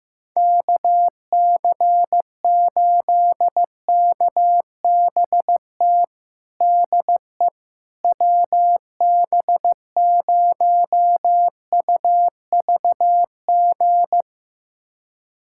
Morse code, heard as a keyed tone, about 16 wpm, 700 Hz. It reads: KC8KBT DE WB0UVG
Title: USA Callsign DE Callsign - 6 Characters - 15wpm